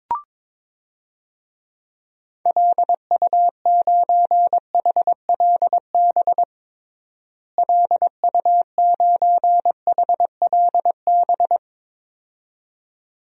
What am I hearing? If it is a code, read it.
Morse code, heard as a keyed tone, about 22 wpm, 700 Hz. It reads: LU9HLB LU9HLB
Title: DXCC-100 Call Signs - 6 Characters - Repeated 2x - 22wpm